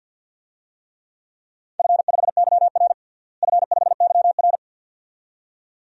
Morse code, heard as a keyed tone, about 48 wpm, 700 Hz. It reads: F5YR F5YR
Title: DXCC-100 Call Signs - 4 Characters - Repeated 2x - 50wpm